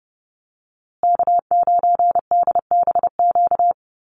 Morse code, heard as a keyed tone, about 30 wpm, 700 Hz. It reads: K9DBQ
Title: USA Call Signs Repeated 3x 30wpm